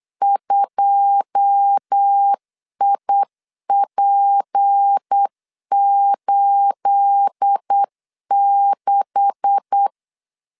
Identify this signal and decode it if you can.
Morse code, heard as a keyed tone, about 8 wpm, 800 Hz. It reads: 2IP86
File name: CW / DET / 1 9 1500C S A 8wpm 5637